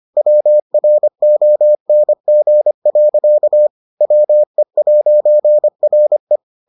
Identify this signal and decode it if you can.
Morse code, heard as a keyed tone, about 25 wpm, 600 Hz. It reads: WRONG. WE'RE